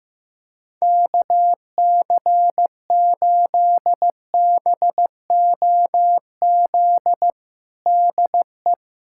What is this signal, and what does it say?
Morse code, heard as a keyed tone, about 15 wpm, 700 Hz. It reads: KC8BOZ DE
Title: USA Callsign DE Callsign - 6 Characters - 15wpm